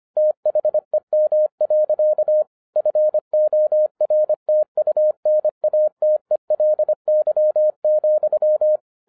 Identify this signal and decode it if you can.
Morse code, heard as a keyed tone, about 25 wpm, 610 Hz. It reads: THEM. FORTUNATELY,
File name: 2012-01-mp3 / n14-FUN WITH NUMBERS-25wpm